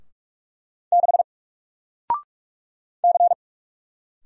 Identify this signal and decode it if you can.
Morse code, heard as a keyed tone, about 44 wpm, 700 Hz. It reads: 6 C
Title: Single Character (Letter:Number:Punctuation:Prosign) 45wpm